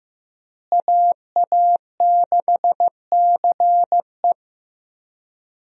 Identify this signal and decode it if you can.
Morse code, heard as a keyed tone, about 15 wpm, 700 Hz. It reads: AA6CE
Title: USA Call Signs - 5 Characters - Repeated 3x - 15wpm